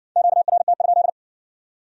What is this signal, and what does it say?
Morse code, heard as a keyed tone, about 57 wpm, 700 Hz. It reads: QRT?